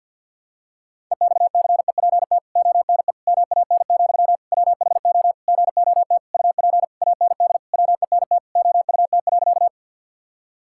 Morse code, heard as a keyed tone, about 52 wpm, 700 Hz. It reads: EXCEPT ONE MAN, WHO GOT UP AND WENT OUT.
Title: Sentences from Top 1600 Words 50wpm